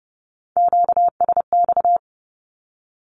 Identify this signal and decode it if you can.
Morse code, heard as a keyed tone, about 30 wpm, 700 Hz. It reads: QSX
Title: Q-Codes Repeated 3x 30wpm